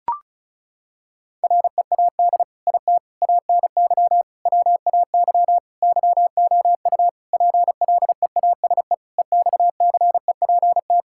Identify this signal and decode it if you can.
Morse code, heard as a keyed tone, about 36 wpm, 700 Hz. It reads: READ IT ANY WAY YOU PLEASE EXCEPT
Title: Sentences from Top 1600 Words 35wpm